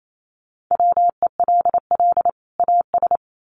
Morse code, heard as a keyed tone, about 28 wpm, 700 Hz. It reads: WELL AS
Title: Sets of 2 Words 28wpm